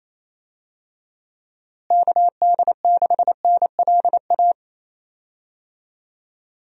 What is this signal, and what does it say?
Morse code, heard as a keyed tone, about 28 wpm, 700 Hz. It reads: KD6NLA
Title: USA Call Signs - 6 Characters - Repeated 2x - 28wpm